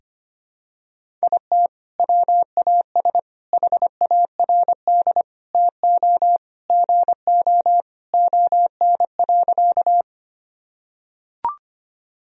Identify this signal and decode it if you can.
Morse code, heard as a keyed tone, about 25 wpm, 700 Hz. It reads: IT WAS HARD TO GO ON.
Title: Sentences from Top 1000 Words - Review Part 1 - 25wpm